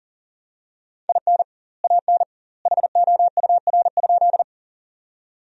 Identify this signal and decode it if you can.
Morse code, heard as a keyed tone, about 40 wpm, 700 Hz. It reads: IN AN HOUR?
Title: 2x Word-Spacing - Sentences from Top 400 Words - 40wpm